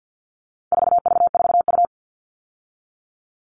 Morse code, heard as a keyed tone, about 50 wpm, 700 Hz. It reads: 4V4U
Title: All Cut Numbers - 4-Digit 50wpm